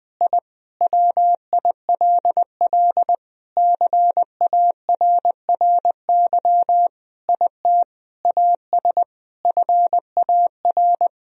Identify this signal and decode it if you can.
Morse code, heard as a keyed tone, about 20 wpm, 700 Hz. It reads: I WILL CARRY IT AS FAR
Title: Sentences from Top 1000 Words - Review Part 1 - 20wpm